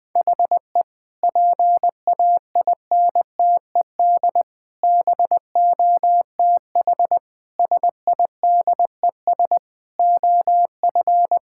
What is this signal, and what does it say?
Morse code, heard as a keyed tone, about 20 wpm, 700 Hz. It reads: HE PAINTED BOTH SIDES OF
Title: Sentences with 4-Word Phrases - 20wpm